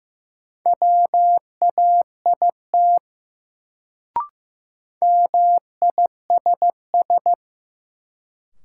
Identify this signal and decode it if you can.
Morse code, heard as a keyed tone, about 15 wpm, 700 Hz. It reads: WAIT MISS
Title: Sentences from Top 500 Words - New Word Drill - 15wpm